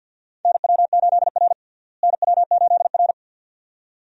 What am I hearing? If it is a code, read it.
Morse code, heard as a keyed tone, about 46 wpm, 700 Hz. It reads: NW8R NW8R
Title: USA Call Signs - 4 Characters - Repeated 2x - 50wpm